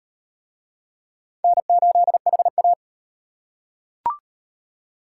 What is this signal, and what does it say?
Morse code, heard as a keyed tone, about 37 wpm, 700 Hz. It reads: N8HA